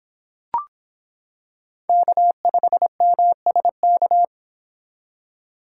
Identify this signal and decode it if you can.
Morse code, heard as a keyed tone, about 25 wpm, 700 Hz. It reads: K5MSK